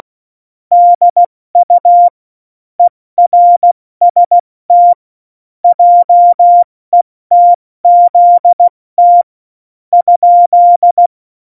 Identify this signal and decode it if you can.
Morse code, heard as a keyed tone, about 16 wpm, spaced as for 13 wpm, 700 Hz. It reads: DU ERST JETZT?
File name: Morsekurs / Witze / Witz 27 March 2018, 70 F80